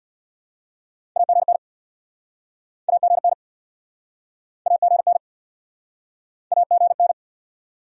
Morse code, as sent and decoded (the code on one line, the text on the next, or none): .- --. -. / .- --. -. / .- --. -. / .- --. -.
AGN AGN AGN AGN